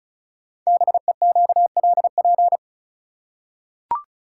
Morse code, heard as a keyed tone, about 36 wpm, 690 Hz. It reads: BEQLP
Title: Code Groups - 5 Letters - 35wpm